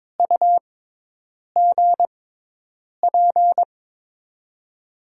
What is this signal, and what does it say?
Morse code, heard as a keyed tone, about 22 wpm, spaced as for 8 wpm, 700 Hz. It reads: UGP
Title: USA Call Signs - 6 Characters - 6x Character Spacing - 22wpm